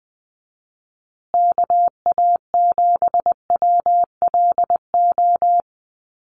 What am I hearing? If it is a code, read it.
Morse code, heard as a keyed tone, about 20 wpm, 700 Hz. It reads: KA7WLO
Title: USA Call Signs Repeated 2x 20wpm